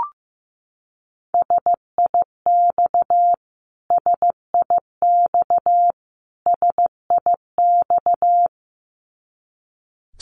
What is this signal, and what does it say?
Morse code, heard as a keyed tone, about 15 wpm, 700 Hz. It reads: SIX SIX SIX
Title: Top 500 Words Repeated 15wpm